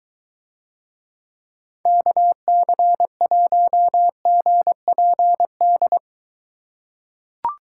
Morse code, heard as a keyed tone, about 22 wpm, 700 Hz. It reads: KC1GPD